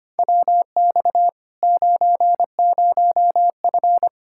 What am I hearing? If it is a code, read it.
Morse code, heard as a keyed tone, about 25 wpm, 700 Hz. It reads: WX 90F
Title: Practice QSO - Exchange 2 - Info - 25wpm